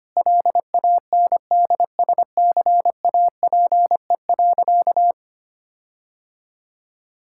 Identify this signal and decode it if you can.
Morse code, heard as a keyed tone, about 25 wpm, 700 Hz. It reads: LANDSCAPE.
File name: practice / 25wpm / Difficult Sentences with Hints - 25wpm